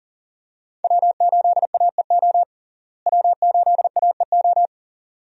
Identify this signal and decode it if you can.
Morse code, heard as a keyed tone, about 40 wpm, 700 Hz. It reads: W8AEO W8AEO
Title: USA Call Signs - 5 Characters - Repeated 2x - 40wpm